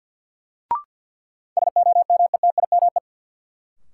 Morse code, heard as a keyed tone, about 48 wpm, 700 Hz. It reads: SOMETIME
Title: Sentences from Top 1000 Words - Review Part 6 - New Word Drill - 50wpm